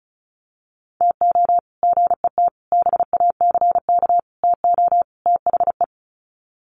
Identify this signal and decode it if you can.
Morse code, heard as a keyed tone, about 34 wpm, 700 Hz. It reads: TO GET BACK TO THE
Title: Sets of 5 Words Encore 35wpm